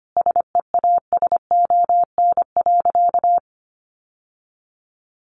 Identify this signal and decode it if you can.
Morse code, heard as a keyed tone, about 25 wpm, 700 Hz. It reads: SEASON.
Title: Sentences from Top 1400 Words 25wpm